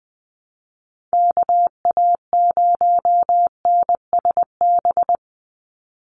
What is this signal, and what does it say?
Morse code, heard as a keyed tone, about 20 wpm, 700 Hz. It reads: KA0NSB
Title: USA Call Signs 20wpm